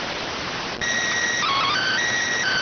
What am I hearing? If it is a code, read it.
SSTV (slow-scan television), PD 290.